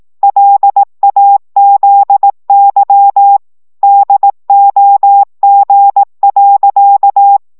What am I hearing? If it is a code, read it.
Morse code, heard as a keyed tone, about 18 wpm, 800 Hz. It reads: LAZY DOG.
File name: szamolo / CW 15 baud